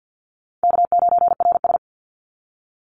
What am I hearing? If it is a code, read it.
Morse code, heard as a keyed tone, about 48 wpm, 700 Hz. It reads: K9RS
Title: Call Signs Repeated 2x - 50wpm